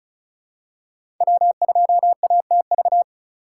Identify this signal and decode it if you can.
Morse code, heard as a keyed tone, about 35 wpm, 700 Hz. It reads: W2ATV
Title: USA Call Signs - 5 Characters - 3x Character Spacing - 35wpm